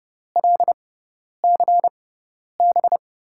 Morse code, heard as a keyed tone, about 30 wpm, spaced as for 10 wpm, 700 Hz. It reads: LCB
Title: USA Call Signs - 6 Characters - 6x Character Spacing - 30wpm